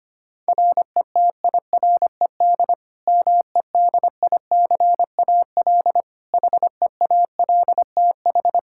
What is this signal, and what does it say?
Morse code, heard as a keyed tone, about 25 wpm, 700 Hz. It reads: RETIRED MEDICAL HEALTH
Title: K1USN SST QSO Practice 25wpm